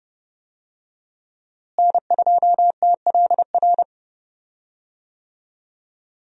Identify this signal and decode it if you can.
Morse code, heard as a keyed tone, about 30 wpm, 700 Hz. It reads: N2TLR
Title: USA Call Signs - 5 Characters - Repeated 2x - 30wpm